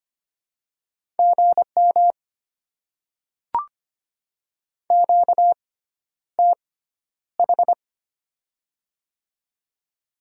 Morse code, heard as a keyed tone, about 25 wpm, spaced as for 11 wpm, 690 Hz. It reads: GM QTH
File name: Top 20 QSO Elements - 6x Character Spacing - 25wpm